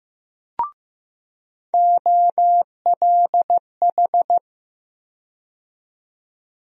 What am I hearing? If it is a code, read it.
Morse code, heard as a keyed tone, about 15 wpm, 700 Hz. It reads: OLH
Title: USA Call Sign Postfixes - 3 Letters 15wpm